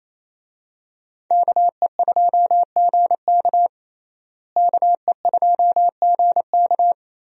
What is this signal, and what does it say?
Morse code, heard as a keyed tone, about 28 wpm, 700 Hz. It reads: KE2GK KE2GK
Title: USA Call Signs - 5 Characters - Repeated 3x - 28wpm